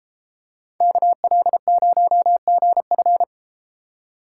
Morse code, heard as a keyed tone, about 32 wpm, 700 Hz. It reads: KL0GF